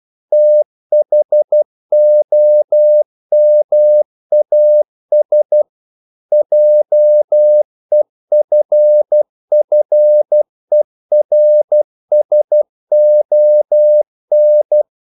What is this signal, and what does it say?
Morse code, heard as a keyed tone, about 12 wpm, 600 Hz. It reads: THOMAS JEFFERSON
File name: CW / Practice / EnglishQuotations0112